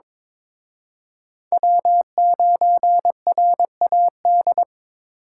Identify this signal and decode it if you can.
Morse code, heard as a keyed tone, about 26 wpm, 700 Hz. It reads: W9RAD
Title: Speed Racing - USA 1x3 Call Signs 22wpm